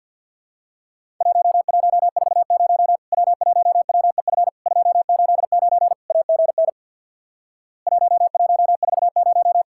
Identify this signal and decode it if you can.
Morse code, heard as a keyed tone, about 50 wpm, 700 Hz. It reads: 1140 W1WEF 289 AGN 1140
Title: Worked All Europe DX Contest - QTC Exchange with Repeats - 50wpm